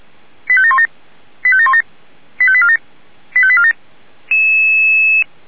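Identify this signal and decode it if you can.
Five-tone selective call (ZVEI), heard as quick runs of five tones, tones of about 70 ms. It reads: 85717 85717 85847 85847
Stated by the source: ZVEI five-tone call received by SDR.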